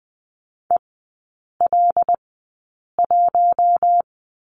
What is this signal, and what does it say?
Morse code, heard as a keyed tone, about 20 wpm, spaced as for 8 wpm, 700 Hz. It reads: EL1
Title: 107 Lesson 18e - Letter m - Three Characters 20wpm